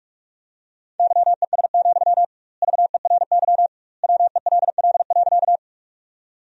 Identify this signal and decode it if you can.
Morse code, heard as a keyed tone, about 48 wpm, 700 Hz. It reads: YES, VERY WELL.
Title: 2x Word-Spacing - Sentences from Top 500 Words - 45wpm